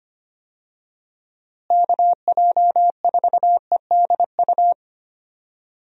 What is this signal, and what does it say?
Morse code, heard as a keyed tone, about 25 wpm, 700 Hz. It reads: KJ4EDU
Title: USA Call Signs - 6 Characters - Repeated 3x - 25wpm